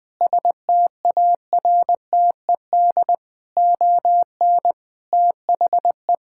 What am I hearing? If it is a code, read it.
Morse code, heard as a keyed tone, about 21 wpm, 700 Hz. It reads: STARTED ON THE
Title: Sentences with 7-Word Phrases - 20wpm